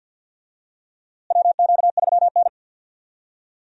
Morse code, heard as a keyed tone, about 52 wpm, 700 Hz. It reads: WQ3N
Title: USA Call Signs - 4 Characters - Repeated 3x - 50wpm